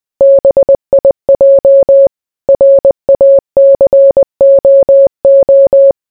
Morse code, heard as a keyed tone, about 20 wpm, 550 Hz. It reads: BIJ RACOO